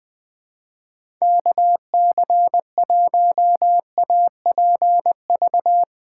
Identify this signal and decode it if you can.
Morse code, heard as a keyed tone, about 20 wpm, 700 Hz. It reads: KC1APV
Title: USA Callsign DE Callsign 20wpm